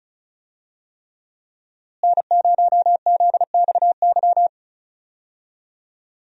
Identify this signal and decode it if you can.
Morse code, heard as a keyed tone, about 34 wpm, 700 Hz. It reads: N0ZXY
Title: USA Call Signs - 5 Characters - Repeated 3x - 35wpm